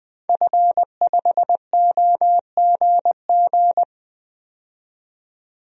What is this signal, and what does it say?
Morse code, heard as a keyed tone, about 21 wpm, 700 Hz. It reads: F5OGG
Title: DXCC-100 Call Signs - 5 Characters - 20wpm